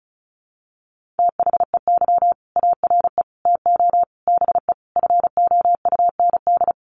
Morse code, heard as a keyed tone, about 34 wpm, 700 Hz. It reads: THEY ARE TO BE FOUND
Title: Sentences from Top 1400 Words 35wpm